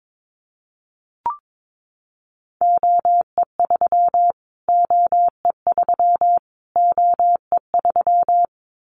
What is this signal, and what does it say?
Morse code, heard as a keyed tone, about 22 wpm, 700 Hz. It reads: OE3 OE3 OE3
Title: Callsign Prefixes for Top 25 DXCC - Repeated 3x - 22wpm